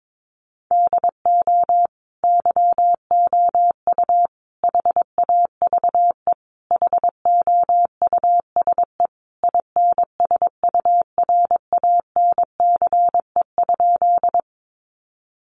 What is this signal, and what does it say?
Morse code, heard as a keyed tone, about 22 wpm, 700 Hz. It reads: DO YOU HAVE HOUSE INSURANCE?
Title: Sentences from Top 1100 Words 22wpm